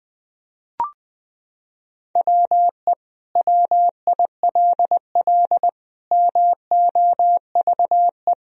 Morse code, heard as a keyed tone, about 21 wpm, 700 Hz. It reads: WE WILL MOVE
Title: Sentences from Top 1000 Words - Review Part 4 - 20wpm